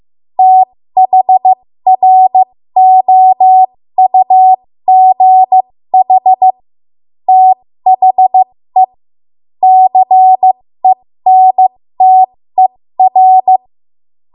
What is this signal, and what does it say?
Morse code, heard as a keyed tone, about 15 wpm, spaced as for 13 wpm, 750 Hz. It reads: THROUGH THE CENTER
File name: Morse / 13WPM